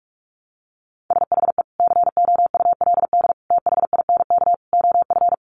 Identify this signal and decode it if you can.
Morse code, heard as a keyed tone, about 44 wpm, 700 Hz. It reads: SHE COULD THINK OF
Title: Sentences from Top 800 Words 45wpm